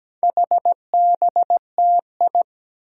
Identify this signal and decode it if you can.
Morse code, heard as a keyed tone, about 17 wpm, 700 Hz. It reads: HBTI
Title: Four Challenging Letter-Number Combinations 17wpm